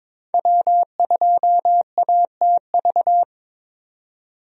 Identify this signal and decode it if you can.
Morse code, heard as a keyed tone, about 22 wpm, 700 Hz. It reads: W2ATV
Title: USA Call Signs - 5 Characters - 22wpm